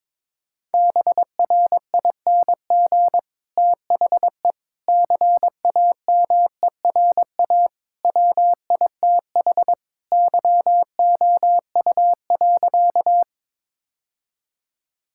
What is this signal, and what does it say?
Morse code, heard as a keyed tone, about 22 wpm, 700 Hz. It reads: BRING THE CAMERA WITH YOU.
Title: Sentences from Top 1000 Words - Review Part 5 - 22wpm